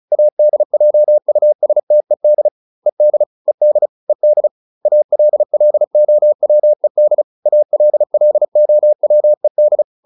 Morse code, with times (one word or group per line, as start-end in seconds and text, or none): 0.1-2.5 ADJUSTED
2.9-3.2 ED
3.5-3.9 ED
4.1-4.5 ED
4.9-7.2 ALLOWED
7.5-9.8 ALLOWED